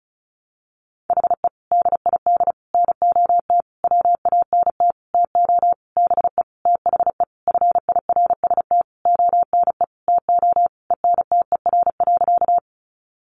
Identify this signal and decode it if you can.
Morse code, heard as a keyed tone, about 34 wpm, 700 Hz. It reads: HE DID NOT WANT TO BE THE FIRST ONE TO ENTER.
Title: Sentences from Top 1300 Words 35wpm